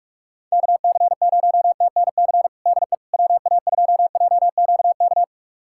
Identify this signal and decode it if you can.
Morse code, heard as a keyed tone, about 46 wpm, 700 Hz. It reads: KC0TNC DE WA2JQK
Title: USA Callsign DE Callsign - 6 Characters - 45wpm